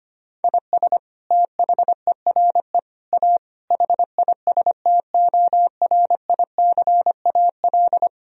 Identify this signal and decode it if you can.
Morse code, heard as a keyed tone, about 25 wpm, 700 Hz. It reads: IS THERE A HISTORICAL
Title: Sentences from Top 1000 Words - Review Part 4 - 25wpm